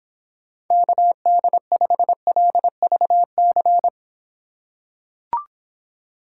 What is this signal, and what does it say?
Morse code, heard as a keyed tone, about 26 wpm, 700 Hz. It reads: KD5LVC